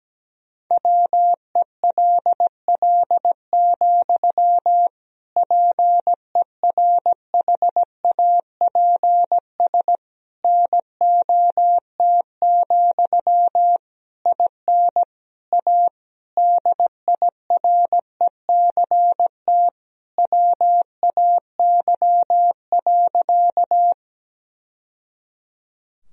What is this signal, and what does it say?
Morse code, heard as a keyed tone, about 17 wpm, 700 Hz. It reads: WELL, PERHAPS NOT, IN A DIRECT WAY.
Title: Sentences from Top 1000 Words - Review Part 2 - 17wpm